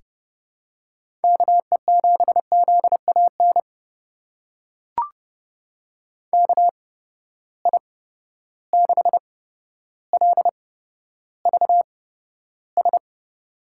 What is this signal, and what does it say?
Morse code, heard as a keyed tone, about 31 wpm, spaced as for 13 wpm, 700 Hz. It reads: KE7ZAN KI6LVS
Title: USA Call Signs - 6 Characters - 8x Character Spacing - 30wpm